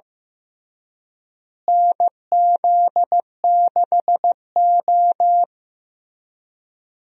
Morse code, heard as a keyed tone, about 16 wpm, 700 Hz. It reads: NZ6O
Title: USA Call Signs - 4 Characters - Repeated 3x - 15wpm